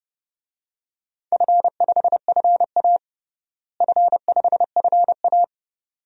Morse code, heard as a keyed tone, about 30 wpm, 700 Hz. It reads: F5FA F5FA
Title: DXCC-100 Call Signs - 4 Characters - Repeated 2x - 30wpm